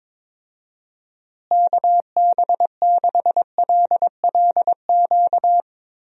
Morse code, heard as a keyed tone, about 22 wpm, 700 Hz. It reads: KB6LLQ
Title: USA Call Signs - 6 Characters - Repeated 3x - 22wpm